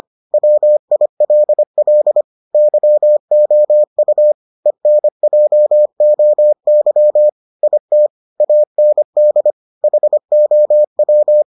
Morse code, heard as a keyed tone, about 25 wpm, 600 Hz. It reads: WILL YOU ENJOY IT AND HOW